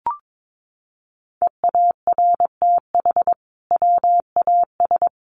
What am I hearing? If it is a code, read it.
Morse code, heard as a keyed tone, about 22 wpm, 700 Hz. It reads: EARTH WAS
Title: Sentences from Top 1200 Words 22wpm